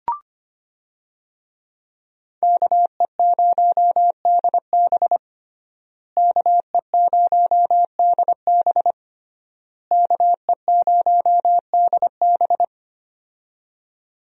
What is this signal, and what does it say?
Morse code, heard as a keyed tone, about 25 wpm, 700 Hz. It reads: KE0DB KE0DB KE0DB
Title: USA Call Signs - 5 Characters - Repeated 3x - 25wpm